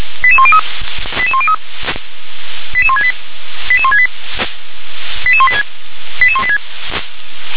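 Five-tone selective call (ZVEI), heard as quick runs of five tones, tones of about 70 ms. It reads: 88103 88103 88168 88168 88186 88186
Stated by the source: Five-tone ZVEI selective call received on SDR.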